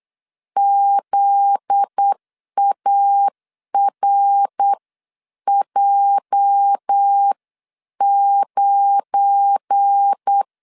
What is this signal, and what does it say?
Morse code, heard as a keyed tone, about 8 wpm, spaced as for 7 wpm, 800 Hz. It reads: ZARJ9
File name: CW / DET / 1 9 1500C S A 8wpm 5637